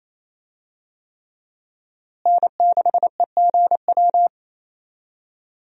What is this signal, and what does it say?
Morse code, heard as a keyed tone, about 28 wpm, 700 Hz. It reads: N6EGW
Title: USA Call Signs - 5 Characters - Repeated 3x - 28wpm